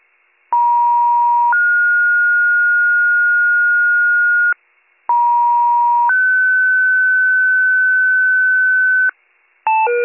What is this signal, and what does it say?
Two-tone page, heard as a one-second tone, then a three-second tone, about 950 then 1430 Hz, about 950 then 1500 Hz, with noise. It is heard from 0.5 to 9.1 s.